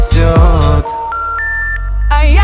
Special information tone, with noise.